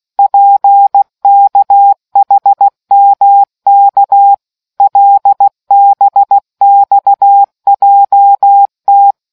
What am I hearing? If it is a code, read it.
Morse code, heard as a keyed tone, about 16 wpm, 800 Hz. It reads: PKHMK LBXJT